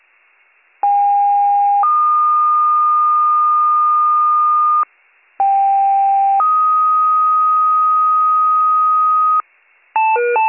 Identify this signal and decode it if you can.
Two-tone page, heard as a one-second tone, then a three-second tone, about 810 then 1220 Hz, about 790 then 1250 Hz. It is heard from 0.8 to 9.4 s.